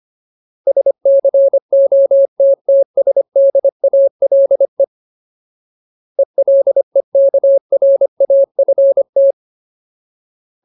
Morse code, heard as a keyed tone, about 25 wpm, 550 Hz. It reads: SCOTTSDALE ELEKRAFT
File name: Ultimate Word List / 25WPM Ultimate